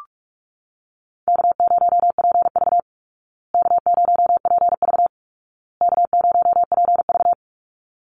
Morse code, heard as a keyed tone, about 44 wpm, 700 Hz. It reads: K0PV K0PV K0PV
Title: Call Signs Repeated 2x - 45wpm